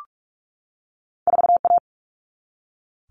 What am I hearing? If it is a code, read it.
Morse code, heard as a keyed tone, about 46 wpm, 700 Hz. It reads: 4A
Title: Common Cut Numbers - 2-Digit 45wpm